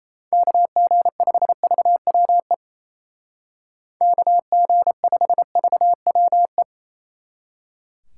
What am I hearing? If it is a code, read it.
Morse code, heard as a keyed tone, about 35 wpm, 700 Hz. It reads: KG5VWE KG5VWE